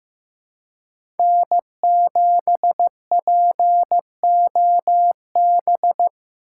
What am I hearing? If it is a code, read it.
Morse code, heard as a keyed tone, about 15 wpm, 700 Hz. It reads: N7POB
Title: USA Call Signs - 4x Character Spacing - 15wpm